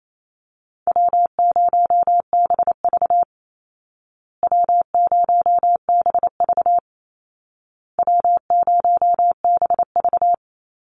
Morse code, heard as a keyed tone, about 28 wpm, 700 Hz. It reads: W0BV W0BV W0BV
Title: Easy Call Signs Repeated 3x - 28wpm